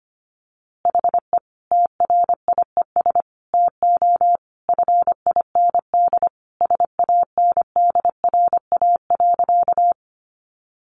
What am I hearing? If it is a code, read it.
Morse code, heard as a keyed tone, about 25 wpm, 700 Hz. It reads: HE TRIES TO FIND SANDRA.
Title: Easy Phrases 25wpm